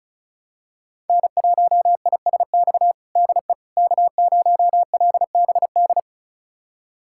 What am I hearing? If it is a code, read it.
Morse code, heard as a keyed tone, about 34 wpm, 700 Hz. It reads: N1ISX DE K0LBD
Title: USA Callsign DE Callsign - 5 Characters - 35wpm